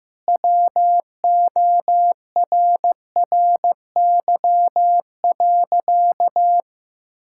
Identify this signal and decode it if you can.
Morse code, heard as a keyed tone, about 15 wpm, 700 Hz. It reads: WORRY.
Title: Pangram Challenge v2 - 15wpm